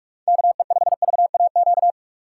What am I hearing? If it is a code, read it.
Morse code, heard as a keyed tone, about 46 wpm, 700 Hz. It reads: KE5VAQ
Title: USA Call Signs - 6 Characters - 4x Character Spacing - 45wpm